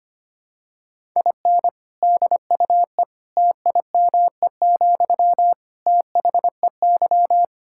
Morse code, heard as a keyed tone, about 25 wpm, 700 Hz. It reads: IN DUE TIME, THEY
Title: Sentences from Top 1000 Words - Review Part 6 - 25wpm